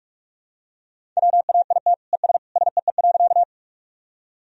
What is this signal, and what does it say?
Morse code, heard as a keyed tone, about 46 wpm, 700 Hz. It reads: WAIT ES SEE.
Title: Sentences from Top 500 Words and Abbreviations 45wpm